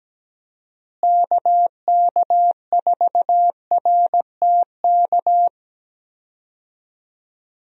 Morse code, heard as a keyed tone, about 17 wpm, 700 Hz. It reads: KK4RTK
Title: USA Call Signs - 6 Characters - Repeated 2x - 17wpm